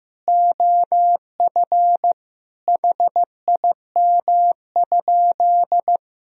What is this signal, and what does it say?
Morse code, heard as a keyed tone, about 15 wpm, 700 Hz. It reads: OF HIM?
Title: Sentences from Top 200 Words and Abbreviations 15wpm